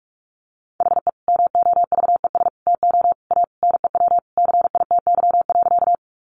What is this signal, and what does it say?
Morse code, heard as a keyed tone, about 44 wpm, 700 Hz. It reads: HE MOVES TO A NEW CITY.
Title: Easy Phrases 45wpm